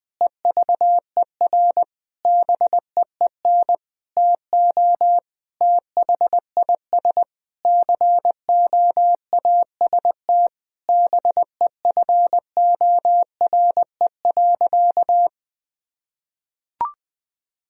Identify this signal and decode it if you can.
Morse code, heard as a keyed tone, about 20 wpm, 700 Hz. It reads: EVER BEEN TO THIS COAST BEFORE.
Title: Sentences from Top 1000 Words - Review Part 4 - 20wpm